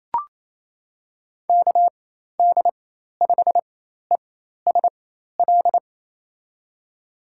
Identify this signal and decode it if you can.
Morse code, heard as a keyed tone, about 28 wpm, spaced as for 13 wpm, 700 Hz. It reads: KD5ESL
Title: USA Call Signs - 6 Characters - 4x Character Spacing - 28wpm